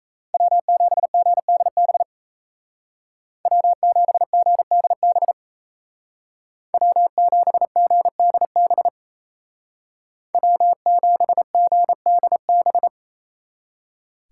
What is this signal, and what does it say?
Morse code, heard as a keyed tone, about 35 wpm, 700 Hz. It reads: W7GDB W7GDB W7GDB W7GDB
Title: Speed Racing - USA Call Signs 28wpm